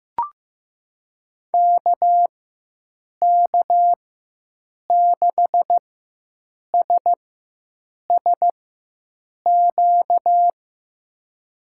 Morse code, heard as a keyed tone, about 15 wpm, spaced as for 7 wpm, 700 Hz. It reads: KK6SSQ
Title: USA Call Signs - 4x Character Spacing - 15wpm